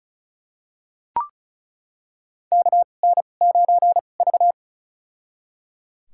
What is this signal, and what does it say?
Morse code, heard as a keyed tone, about 36 wpm, spaced as for 25 wpm, 700 Hz. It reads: KN9V
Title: USA Call Signs - 4 Characters - 2x Character Spacing - 35wpm